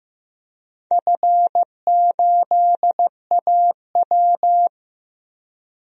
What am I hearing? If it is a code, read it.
Morse code, heard as a keyed tone, about 15 wpm, 700 Hz. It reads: F8AW
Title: DXCC-100 Call Signs - 4 Characters - 15wpm